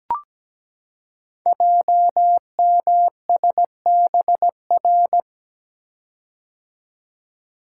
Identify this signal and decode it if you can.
Morse code, heard as a keyed tone, about 17 wpm, 700 Hz. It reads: JMSBR
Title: Code Groups - 5 Letters - 17wpm